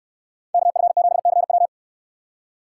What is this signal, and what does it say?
Morse code, heard as a keyed tone, about 68 wpm, 700 Hz. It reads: KF7CW